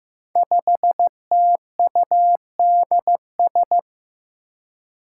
Morse code, heard as a keyed tone, about 15 wpm, 700 Hz. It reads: 5TUDS
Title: Five Challenging Letter-Number Combinations 15wpm